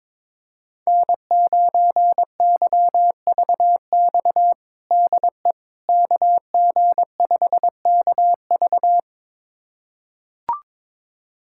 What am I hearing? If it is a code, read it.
Morse code, heard as a keyed tone, about 22 wpm, 700 Hz. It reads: N9YVX DE KG5KV
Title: USA Callsign DE Callsign - 5 Characters - 22wpm